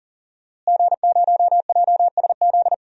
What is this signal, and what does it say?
Morse code, heard as a keyed tone, about 40 wpm, 700 Hz. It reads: G0JSZ
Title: DXCC-100 Call Signs - 5 Characters - Repeated 2x - 40wpm